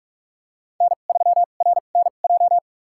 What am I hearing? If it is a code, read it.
Morse code, heard as a keyed tone, about 45 wpm, spaced as for 32 wpm, 700 Hz. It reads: N3RNJ
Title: USA Call Signs - 5 Characters - 2x Character Spacing - 45wpm